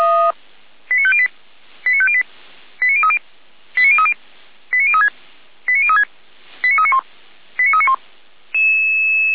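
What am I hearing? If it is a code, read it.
Five-tone selective call (ZVEI), heard as quick runs of five tones, tones of about 70 ms. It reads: 89508 89508 89030 89030 89036 89036 89391 89391
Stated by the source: ZVEI five-tone selective call 89042 received on an SDR.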